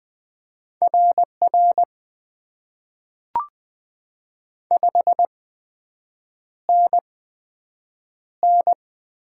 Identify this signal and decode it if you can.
Morse code, heard as a keyed tone, about 20 wpm, spaced as for 7 wpm, 700 Hz. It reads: RR 5NN
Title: Top 90 QSO Elements - 8x Character Spacing - 20wpm